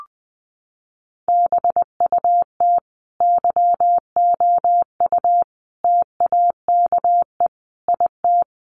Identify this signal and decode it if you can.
Morse code, heard as a keyed tone, about 20 wpm, 700 Hz. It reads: BUT YOU TAKE IT
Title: Sentences from Top 1200 Words 20wpm